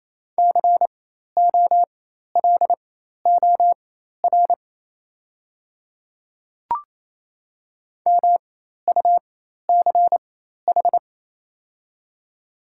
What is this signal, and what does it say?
Morse code, heard as a keyed tone, about 27 wpm, spaced as for 13 wpm, 700 Hz. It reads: COLOR MUCH
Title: 4x Character Spacing - Top 300 Words - 28wpm